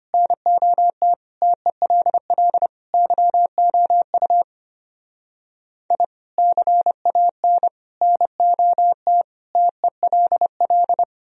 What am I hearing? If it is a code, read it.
Morse code, heard as a keyed tone, about 31 wpm, 700 Hz. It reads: NOT TELL YOU I CAN NOT TELL
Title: Speed-Racing - Sets of 5 Words 25wpm